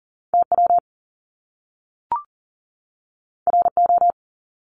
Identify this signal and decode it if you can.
Morse code, heard as a keyed tone, about 40 wpm, 700 Hz. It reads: TW RO
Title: Top Two Letter Combinations in Words 40wpm